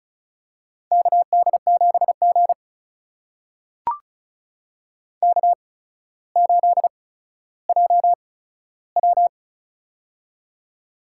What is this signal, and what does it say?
Morse code, heard as a keyed tone, about 34 wpm, spaced as for 15 wpm, 700 Hz. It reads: KD7G K8JW
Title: USA Call Signs - 8x Character Spacing - 35wpm